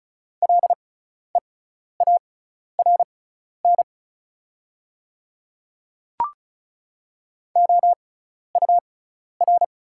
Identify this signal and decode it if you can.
Morse code, heard as a keyed tone, about 36 wpm, spaced as for 12 wpm, 700 Hz. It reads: LEARN OUR
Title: 6x Character Spacing - Top 500 Words - 35wpm